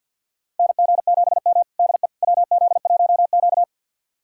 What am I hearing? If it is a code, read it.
Morse code, heard as a keyed tone, about 52 wpm, 700 Hz. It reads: NG7M DE WZ1Q
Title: USA Callsign DE Callsign - 4 Characters - 50wpm